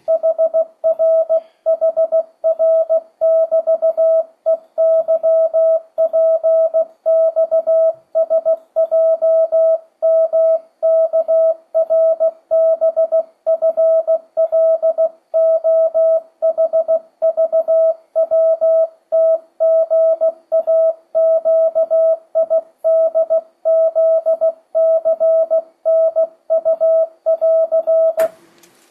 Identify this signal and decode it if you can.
Morse code, heard as a keyed tone, about 16 wpm, 650 Hz. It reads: HRHR<BT>EYPXSJMKRBFLOHVWTGAQIDZCNU<AR>
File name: 2025cw / speedup / 16wpm